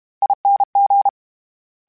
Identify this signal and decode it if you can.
Morse code, heard as a keyed tone, about 32 wpm, 800 Hz. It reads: ING
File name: General / General-mp3s / ESE ING @30